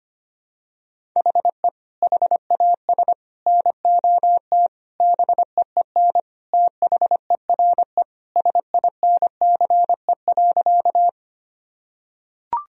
Morse code, heard as a keyed tone, about 25 wpm, 700 Hz. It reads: HE HAS NOT BEEN THERE SINCE.
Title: Sentences from Top 1000 Words - Review Part 1 - 25wpm